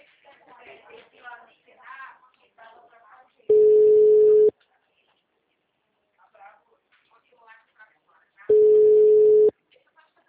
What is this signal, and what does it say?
Ringback tone (Europe, 425 Hz), with speech.